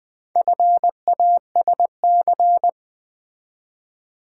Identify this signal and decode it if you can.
Morse code, heard as a keyed tone, about 20 wpm, 700 Hz. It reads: FASC
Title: Build A Word - 4 Letters At A Time - 11-Letter Words - 20wpm